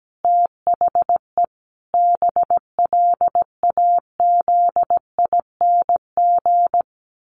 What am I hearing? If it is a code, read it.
Morse code, heard as a keyed tone, about 17 wpm, 700 Hz. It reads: THE BLAZING
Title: A Sound Of Thunder 17wpm